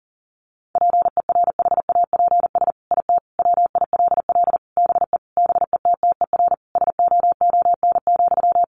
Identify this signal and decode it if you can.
Morse code, heard as a keyed tone, about 40 wpm, 700 Hz. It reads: PERHAPS IT WILL BE BETTER SOON,
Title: Sentences from Top 1500 Words 40wpm